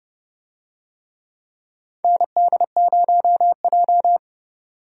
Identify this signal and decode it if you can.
Morse code, heard as a keyed tone, about 30 wpm, 700 Hz. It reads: ND0J
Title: USA Call Signs - 4 Characters - 30wpm